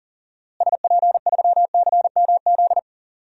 Morse code, heard as a keyed tone, about 40 wpm, 700 Hz. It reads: SP3CMZ
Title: DXCC-100 Call Signs - 6 Characters - Repeated 2x - 40wpm